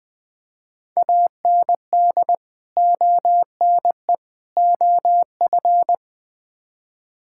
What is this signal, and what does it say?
Morse code, heard as a keyed tone, about 20 wpm, 700 Hz. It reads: AND ONE OF
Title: Build A Phrase - 4-Word Phrases - 20wpm